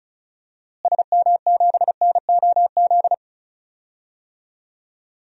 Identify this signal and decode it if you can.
Morse code, heard as a keyed tone, about 34 wpm, 700 Hz. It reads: SM7NOZ
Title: DXCC-100 Call Signs - 6 Characters - Repeated 3x - 35wpm